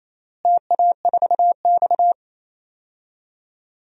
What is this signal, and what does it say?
Morse code, heard as a keyed tone, about 28 wpm, 700 Hz. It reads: TA4X
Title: DXCC-100 Call Signs - 4 Characters - Repeated 3x - 28wpm